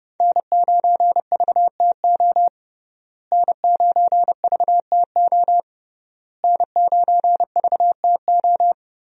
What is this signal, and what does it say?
Morse code, heard as a keyed tone, about 29 wpm, 700 Hz. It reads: N9VTO N9VTO N9VTO
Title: USA Call Signs - 5 Characters - Repeated 3x - 30wpm